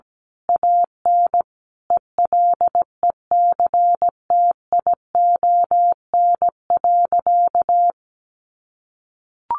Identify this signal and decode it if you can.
Morse code, heard as a keyed tone, about 17 wpm, 700 Hz. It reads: AN ELECTION.
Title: Sentences from Top 1500 Words 17wpm